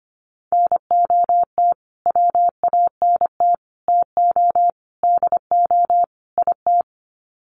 Morse code, heard as a keyed tone, about 25 wpm, 700 Hz. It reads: NOT WANT TO DO IT
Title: Sets of 5 Words Encore 25wpm